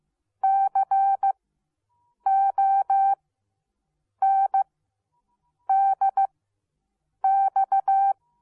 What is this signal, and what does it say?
Morse code, heard as a keyed tone, about 15 wpm, spaced as for 7 wpm, 780 Hz. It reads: CONDX